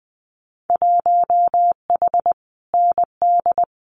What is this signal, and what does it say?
Morse code, heard as a keyed tone, about 20 wpm, 700 Hz. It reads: 1H ND
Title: Winter Field Day Exchanges 20wpm